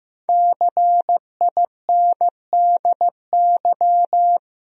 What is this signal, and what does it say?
Morse code, heard as a keyed tone, about 16 wpm, 700 Hz. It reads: CINDY
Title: USA Common Names - Max 5 Letters - 15wpm